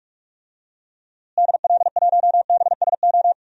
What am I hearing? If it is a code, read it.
Morse code, heard as a keyed tone, about 46 wpm, 700 Hz. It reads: DL1BSO
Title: DXCC-100 Call Signs - 6 Characters - Repeated 2x - 45wpm